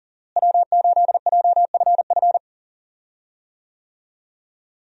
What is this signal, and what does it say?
Morse code, heard as a keyed tone, about 40 wpm, 700 Hz. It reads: W8JFF
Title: USA Call Signs - 5 Characters - Repeated 3x - 40wpm